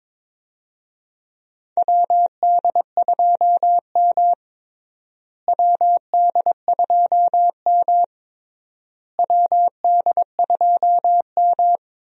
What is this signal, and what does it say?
Morse code, heard as a keyed tone, about 22 wpm, 700 Hz. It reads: WD2M WD2M WD2M
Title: USA Call Signs - 4 Characters - Repeated 3x - 22wpm